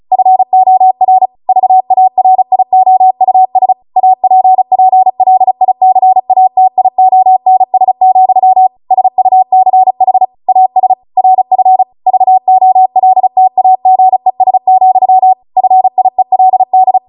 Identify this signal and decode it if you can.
Morse code, heard as a keyed tone, about 36 wpm, 750 Hz. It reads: FOR VARIOUS APPLICATIONS, SUCH AS RF VOLTAGES, FIELD